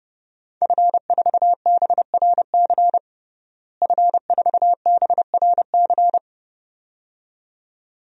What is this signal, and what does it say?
Morse code, heard as a keyed tone, about 32 wpm, 700 Hz. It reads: F4BRC F4BRC
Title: DXCC-100 Call Signs - 5 Characters - Repeated 2x - 30wpm